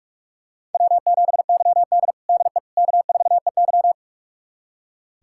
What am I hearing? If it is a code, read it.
Morse code, heard as a keyed tone, about 46 wpm, 700 Hz. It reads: W7YD DE K4EY
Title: USA Callsign DE Callsign - 4 Characters - 45wpm